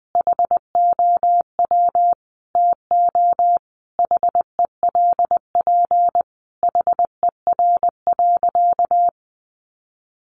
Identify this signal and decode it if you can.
Morse code, heard as a keyed tone, about 20 wpm, 700 Hz. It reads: HOW TO HELP HER.
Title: Sentences from Top 1000 Words 20wpm